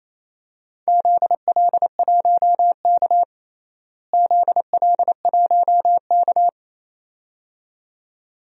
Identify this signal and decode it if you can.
Morse code, heard as a keyed tone, about 28 wpm, 700 Hz. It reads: ZL1K ZL1K
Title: DXCC-100 Call Signs - 4 Characters - Repeated 2x - 28wpm